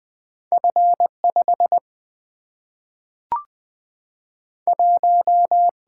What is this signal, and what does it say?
Morse code, heard as a keyed tone, about 20 wpm, 700 Hz. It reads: F 5 1
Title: Single-Letter Single-Number 20wpm